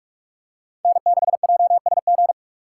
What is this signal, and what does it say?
Morse code, heard as a keyed tone, about 46 wpm, 700 Hz. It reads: N6JSG
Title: USA Call Signs - 5 Characters - 4x Character Spacing - 45wpm